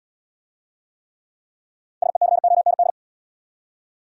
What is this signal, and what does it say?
Morse code, heard as a keyed tone, about 64 wpm, 700 Hz. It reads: SECOND